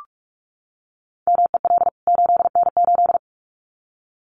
Morse code, heard as a keyed tone, about 44 wpm, 700 Hz. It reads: MEL 8N8
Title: CWT Exchange 45wpm